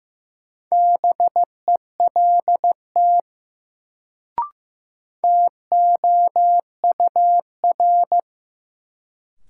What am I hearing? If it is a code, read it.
Morse code, heard as a keyed tone, about 15 wpm, 700 Hz. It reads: BELT TOUR
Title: Sentences from Top 1400 Words - New Word Drill Alt - 15wpm